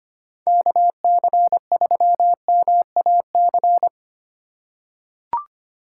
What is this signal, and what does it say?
Morse code, heard as a keyed tone, about 25 wpm, 700 Hz. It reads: KC3MAC
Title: USA Call Signs - 6 Characters - 2x Character Spacing - 25wpm